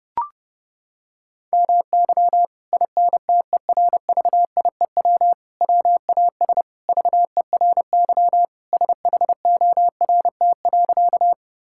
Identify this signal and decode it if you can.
Morse code, heard as a keyed tone, about 32 wpm, 700 Hz. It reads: MY INTERVIEW WAS VERY SHORT.
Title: Sentences from Top 1600 Words 30wpm